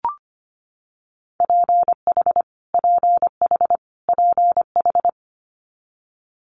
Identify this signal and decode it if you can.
Morse code, heard as a keyed tone, about 25 wpm, 700 Hz. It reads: PH PH PH
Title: Multi-letter Phonemes Repeated 3x 25wpm